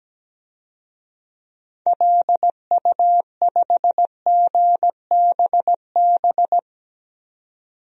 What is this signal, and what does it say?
Morse code, heard as a keyed tone, about 17 wpm, 700 Hz. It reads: LU5GBB
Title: DXCC-100 Call Signs - 6 Characters - 17wpm